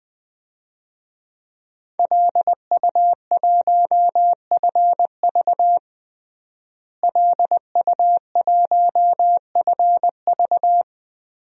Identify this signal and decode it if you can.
Morse code, heard as a keyed tone, about 20 wpm, 700 Hz. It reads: LU1FV LU1FV
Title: DXCC-100 Call Signs - 5 Characters - Repeated 3x - 20wpm